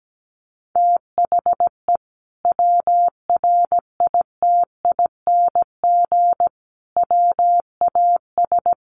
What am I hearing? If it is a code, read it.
Morse code, heard as a keyed tone, about 17 wpm, 700 Hz. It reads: THE WRITING WAS
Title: Sentences from Top 1100 Words 17wpm